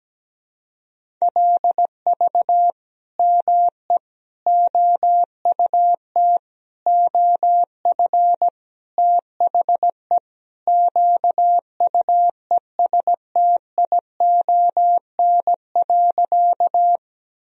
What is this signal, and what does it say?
Morse code, heard as a keyed tone, about 17 wpm, 700 Hz. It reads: LV ME OUT OF THE QUESTION.
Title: Sentences from Top 400 Words and Abbreviations 17wpm